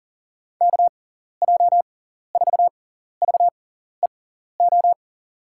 Morse code, heard as a keyed tone, about 37 wpm, spaced as for 14 wpm, 700 Hz. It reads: KJ4VEO
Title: USA Call Signs - 6x Character Spacing - 40wpm